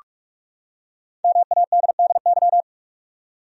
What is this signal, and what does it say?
Morse code, heard as a keyed tone, about 46 wpm, 700 Hz. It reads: MADDY
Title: USA Common Names - 45wpm